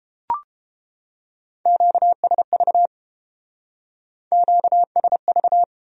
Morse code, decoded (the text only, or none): QSV QSV